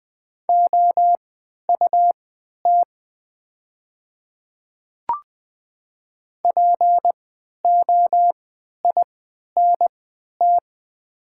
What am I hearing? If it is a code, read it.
Morse code, heard as a keyed tone, about 20 wpm, spaced as for 11 wpm, 700 Hz. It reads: OUT POINT
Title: 3x Character Spacing - Top 200 Words - 20wpm